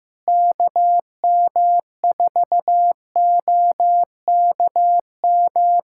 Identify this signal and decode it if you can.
Morse code, heard as a keyed tone, about 15 wpm, 700 Hz. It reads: KM4OKM
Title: USA Callsign DE Callsign 15wpm